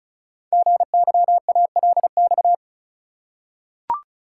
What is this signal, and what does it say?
Morse code, heard as a keyed tone, about 36 wpm, 700 Hz. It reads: GYALX
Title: Code Groups - 5 Letters - 35wpm